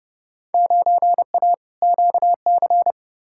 Morse code, heard as a keyed tone, about 30 wpm, 700 Hz. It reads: 9A QC
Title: Field Day Exchanges Encore - 30wpm